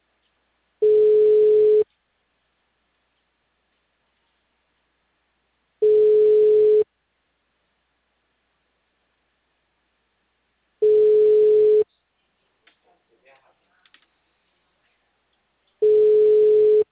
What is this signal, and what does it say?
Ringback tone (Europe, 425 Hz), with speech.